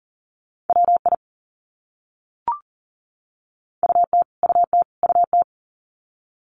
Morse code, heard as a keyed tone, about 40 wpm, 700 Hz. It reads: WI UT UT UT
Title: US States Repeated 3x 40wpm